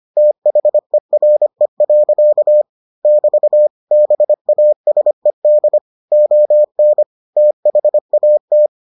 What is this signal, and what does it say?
Morse code, heard as a keyed tone, about 25 wpm, 600 Hz. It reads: THERE. <BT> BASED ON THAT